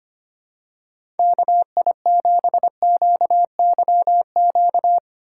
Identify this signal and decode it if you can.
Morse code, heard as a keyed tone, about 25 wpm, 700 Hz. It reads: KI7QYQ
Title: USA Call Signs - 6 Characters - 3x Character Spacing - 25wpm